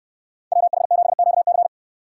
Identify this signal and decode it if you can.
Morse code, heard as a keyed tone, about 64 wpm, 700 Hz. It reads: WH7QC